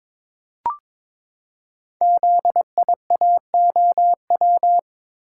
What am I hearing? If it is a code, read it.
Morse code, heard as a keyed tone, about 22 wpm, 700 Hz. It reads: ZIAOW
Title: Code Groups - 5 Letter:Numbers - 22wpm